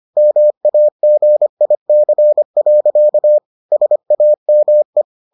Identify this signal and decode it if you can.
Morse code, heard as a keyed tone, about 25 wpm, 600 Hz. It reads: MAGIC. SAME